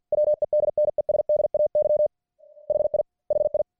Morse code, heard as a keyed tone, about 50 wpm, 600 Hz. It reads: WEDNESDAY HI HI